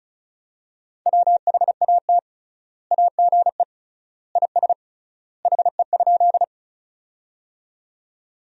Morse code, heard as a keyed tone, about 34 wpm, 700 Hz. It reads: WHAT AGE IS HE?
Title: 3x Word-Spacing - Sentences from Top 500 Words - 35wpm